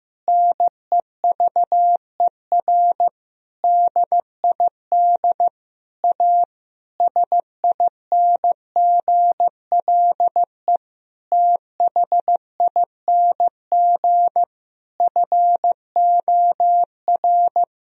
Morse code, heard as a keyed tone, about 15 wpm, 700 Hz. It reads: NEVER DID A SINGLE THING FOR
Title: Sentences from Top 1000 Words - Review Part 4 - 15wpm